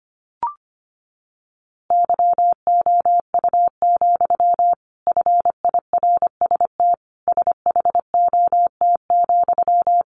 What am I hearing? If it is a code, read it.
Morse code, heard as a keyed tone, about 25 wpm, 700 Hz. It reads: YOU, FIRST SHOT,
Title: A Sound Of Thunder 25wpm